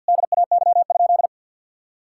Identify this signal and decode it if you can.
Morse code, heard as a keyed tone, about 50 wpm, 700 Hz. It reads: DAY?
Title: Sentences from Top 1000 Words - Review Part 5 - 50wpm